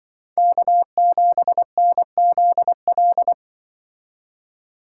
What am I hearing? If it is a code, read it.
Morse code, heard as a keyed tone, about 24 wpm, 700 Hz. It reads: K7NZL
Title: Speed Racing - USA Call Signs 20wpm